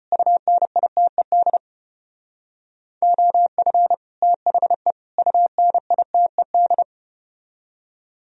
Morse code, heard as a keyed tone, about 34 wpm, 700 Hz. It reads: UNITED OF THE UNITED